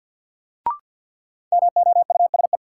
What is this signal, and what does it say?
Morse code, heard as a keyed tone, about 52 wpm, 700 Hz. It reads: MOUSE
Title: Sentences from Top 700 Words - New Word Drill - 50wpm